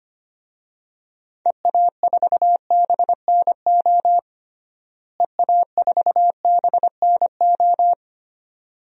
Morse code, heard as a keyed tone, about 25 wpm, 700 Hz. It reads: EA4BNO EA4BNO
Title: DXCC-100 Call Signs - 6 Characters - Repeated 2x - 25wpm